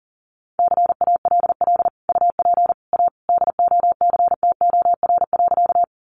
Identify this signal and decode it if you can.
Morse code, heard as a keyed tone, about 40 wpm, 700 Hz. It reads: CALL UP A DOCTOR.
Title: Sentences from Top 1500 Words 40wpm